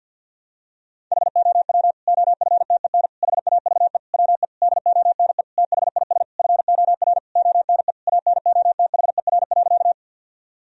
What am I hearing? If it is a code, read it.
Morse code, heard as a keyed tone, about 49 wpm, 700 Hz. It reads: HOW OFTEN HAVE WE DONE THIS FOR ONE ANOTHER.
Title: Sentences from Top 1000 Words - Review Part 1 - 50wpm